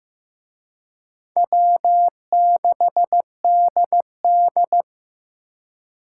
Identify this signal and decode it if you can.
Morse code, heard as a keyed tone, about 18 wpm, 700 Hz. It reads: W6DD
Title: Speed Racing - USA 1x2 Call Signs 15wpm